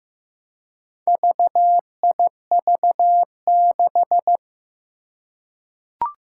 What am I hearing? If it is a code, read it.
Morse code, heard as a keyed tone, about 15 wpm, 700 Hz. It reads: VIV6
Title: Four Challenging Letter-Number Combinations 15wpm